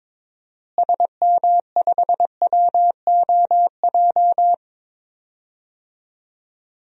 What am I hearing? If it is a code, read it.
Morse code, heard as a keyed tone, about 22 wpm, 700 Hz. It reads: SM5WOJ
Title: DXCC-100 Call Signs - 6 Characters - Repeated 3x - 22wpm